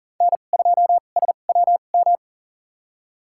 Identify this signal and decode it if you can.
Morse code, heard as a keyed tone, about 40 wpm, spaced as for 29 wpm, 700 Hz. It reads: N2SWM
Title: USA Call Signs - 5 Characters - 2x Character Spacing - 40wpm